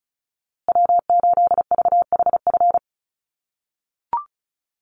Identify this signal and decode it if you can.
Morse code, heard as a keyed tone, about 34 wpm, 700 Hz. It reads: W8VHF
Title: USA Call Signs Repeated 2x 35wpm